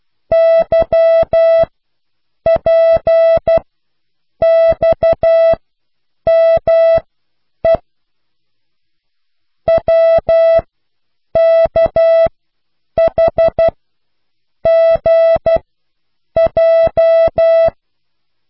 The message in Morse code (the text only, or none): YPXME WKHGJ